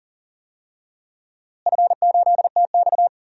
Morse code, heard as a keyed tone, about 40 wpm, 700 Hz. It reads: F8TX
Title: DXCC-100 Call Signs - 4 Characters - Repeated 2x - 40wpm